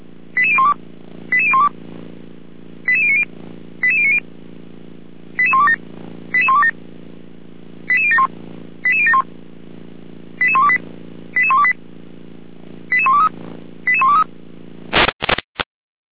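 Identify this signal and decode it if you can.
Five-tone selective call (ZVEI), heard as quick runs of five tones, tones of about 70 ms. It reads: 88012 88012 88090 88090 88127 88127 88071 88071 88128 88128 88123 88123
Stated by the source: ZVEI five-tone selective call received by SDR.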